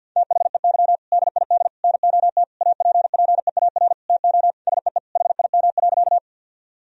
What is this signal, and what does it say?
Morse code, heard as a keyed tone, about 50 wpm, 700 Hz. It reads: THEY DID NOT APPEAR TO SEE HIM.
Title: Sentences from Top 1000 Words - Review Part 2 - 50wpm